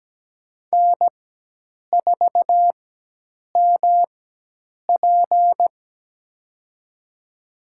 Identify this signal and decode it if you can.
Morse code, heard as a keyed tone, about 17 wpm, spaced as for 8 wpm, 700 Hz. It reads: N4MP
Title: USA Call Signs - 4x Character Spacing - 17wpm